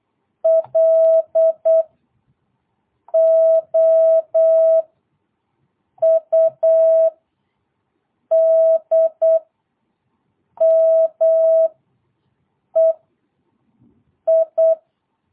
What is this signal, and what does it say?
Morse code, heard as a keyed tone, about 8 wpm, 660 Hz. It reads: L O U D M E I
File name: jh8cbh / 2022cw